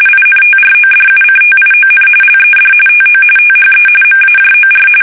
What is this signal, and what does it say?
RTTY, about 45.45 baud. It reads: INTRODUCTION LED TO THE MAJOR
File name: fsk850 / battleshipnj